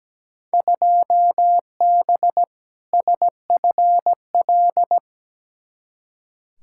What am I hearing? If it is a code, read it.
Morse code, heard as a keyed tone, about 17 wpm, 700 Hz. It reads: 2B SFL
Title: Field Day Exchanges Encore - 17wpm